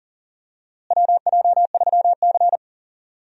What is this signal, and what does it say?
Morse code, heard as a keyed tone, about 40 wpm, 700 Hz. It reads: WJ3C
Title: USA Call Signs - 4 Characters - Repeated 2x - 40wpm